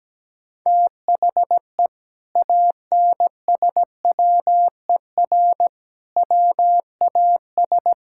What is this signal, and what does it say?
Morse code, heard as a keyed tone, about 17 wpm, 700 Hz. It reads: THE ANSWER WAS
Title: Sentences from Top 1000 Words - Review Part 6 - 17wpm